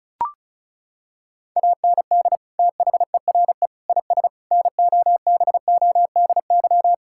Morse code, heard as a keyed tone, about 36 wpm, 700 Hz. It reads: AND THERE IS NOBODY
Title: Sentences from Top 1600 Words 35wpm